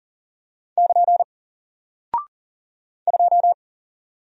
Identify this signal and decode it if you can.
Morse code, heard as a keyed tone, about 38 wpm, 700 Hz. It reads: <KN> 2
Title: Single Character (Letter:Number:Punctuation:Prosign) Alt 40wpm